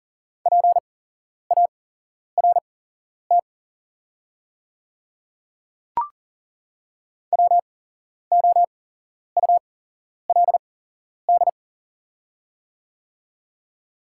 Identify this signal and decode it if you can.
Morse code, heard as a keyed tone, about 40 wpm, spaced as for 11 wpm, 700 Hz. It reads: PART WOULD
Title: 8x Character Spacing - Top 300 Words - 40wpm